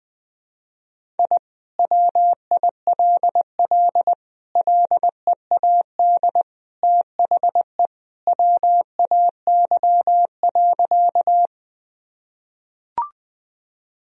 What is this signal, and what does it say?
Morse code, heard as a keyed tone, about 20 wpm, 700 Hz. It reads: I WILL LEAD THE WAY.
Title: Sentences from Top 1000 Words - Review Part 2 - 20wpm